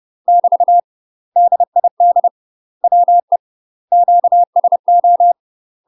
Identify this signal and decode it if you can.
Morse code, heard as a keyed tone, about 30 wpm, 700 Hz. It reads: <BT> DID WE QSO
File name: UK MASTER QSOs TRAINING / 30WPM / UK MASTER QSOs 0036